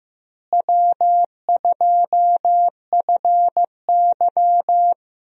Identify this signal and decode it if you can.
Morse code, heard as a keyed tone, about 15 wpm, 700 Hz. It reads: W2FY
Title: USA Call Signs - 4 Characters - Repeated 3x - 15wpm